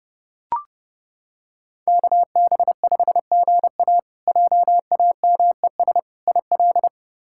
Morse code, heard as a keyed tone, about 30 wpm, 700 Hz. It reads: KB5GA JAMES IL
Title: K1USN SST QSO Practice - 30wpm